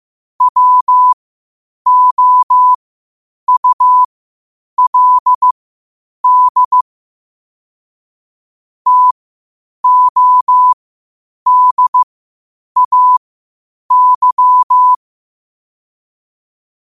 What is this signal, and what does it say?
Morse code, heard as a keyed tone, about 15 wpm, spaced as for 8 wpm, 1000 Hz. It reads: WOULD TODAY
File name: qotdfiles / 8wpm